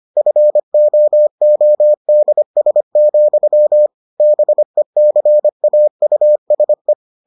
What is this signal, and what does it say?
Morse code, heard as a keyed tone, about 25 wpm, 600 Hz. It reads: FOODS, BECAUSE